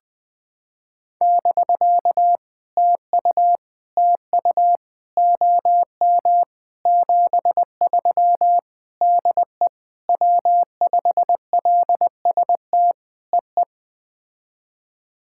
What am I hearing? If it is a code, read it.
Morse code, heard as a keyed tone, about 19 wpm, 700 Hz. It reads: <BK> TU TU OM 73 DE W5LST EE
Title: POTA QSO Practice - 20wpm